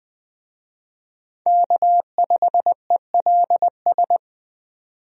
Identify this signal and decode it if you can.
Morse code, heard as a keyed tone, about 20 wpm, 700 Hz. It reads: K5ELS
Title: USA Call Signs - 5 Characters - 20wpm